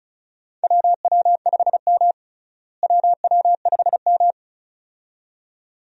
Morse code, heard as a keyed tone, about 34 wpm, 700 Hz. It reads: WW5M WW5M
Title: USA Call Signs - 4 Characters - Repeated 2x - 35wpm